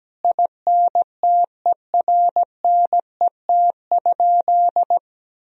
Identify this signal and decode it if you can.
Morse code, heard as a keyed tone, about 17 wpm, 700 Hz. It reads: INTERNET?
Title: Sentences from Top 1000 Words - Review Part 4 - 17wpm